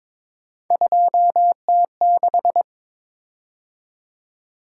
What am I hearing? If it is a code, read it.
Morse code, heard as a keyed tone, about 22 wpm, 700 Hz. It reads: 2T6
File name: practice / 22wpm / Three Letter-Number Combinations - 22wpm